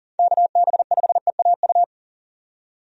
Morse code, heard as a keyed tone, about 40 wpm, 700 Hz. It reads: KB5EAU
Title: USA Call Signs - 6 Characters - 40wpm